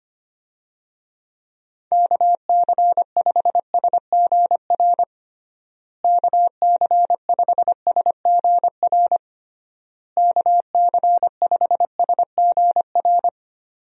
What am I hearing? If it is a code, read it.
Morse code, heard as a keyed tone, about 25 wpm, 700 Hz. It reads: KC5SGR KC5SGR KC5SGR
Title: USA Call Signs - 6 Characters - Repeated 3x - 25wpm